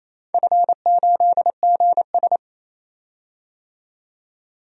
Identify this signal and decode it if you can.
Morse code, heard as a keyed tone, about 28 wpm, 700 Hz. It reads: F8GS
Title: DXCC-100 Call Signs - 4 Characters - 28wpm